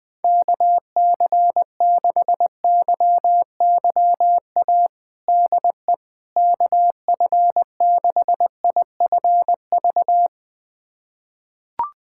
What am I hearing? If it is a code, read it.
Morse code, heard as a keyed tone, about 20 wpm, 700 Hz. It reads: KC6YYA DE KF6IFV
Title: USA Callsign DE Callsign - 6 Characters - 20wpm